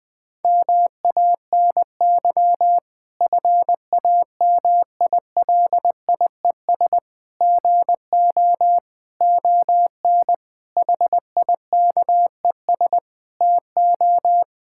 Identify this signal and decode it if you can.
Morse code, heard as a keyed tone, about 21 wpm, 700 Hz. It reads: MANY FAMILIES GO ON HIKES TO
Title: Difficult Sentences with Hints - 20wpm